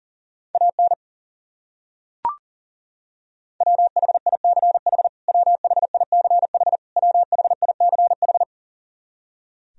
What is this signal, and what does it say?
Morse code, heard as a keyed tone, about 40 wpm, 700 Hz. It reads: AN WHICH WHICH WHICH
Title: Top 100 Words Repeated 40wpm